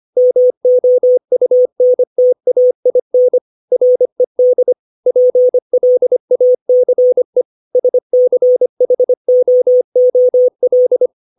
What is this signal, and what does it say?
Morse code, heard as a keyed tone, about 25 wpm, 500 Hz. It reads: MOUNTAIN RED PLACE SCHOOL
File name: 632 words / 632 Words 25WPM